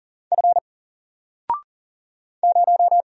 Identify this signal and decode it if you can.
Morse code, heard as a keyed tone, about 39 wpm, 700 Hz. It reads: F 0
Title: Single Character (Letter:Number:Punctuation:Prosign) Alt 40wpm